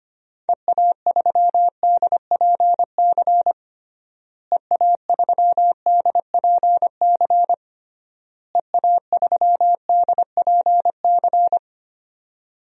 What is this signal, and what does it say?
Morse code, heard as a keyed tone, about 25 wpm, 700 Hz. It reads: EA3DPC EA3DPC EA3DPC
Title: DXCC-100 Call Signs - 6 Characters - Repeated 3x - 25wpm